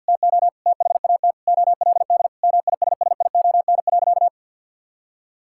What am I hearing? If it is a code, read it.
Morse code, heard as a keyed tone, about 52 wpm, 700 Hz. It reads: TO THAT OLD MISSION.
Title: Sentences from Top 1600 Words 50wpm